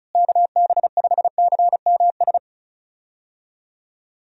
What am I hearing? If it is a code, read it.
Morse code, heard as a keyed tone, about 36 wpm, 700 Hz. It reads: KB5CMS
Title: USA Call Signs - 6 Characters - Repeated 3x - 35wpm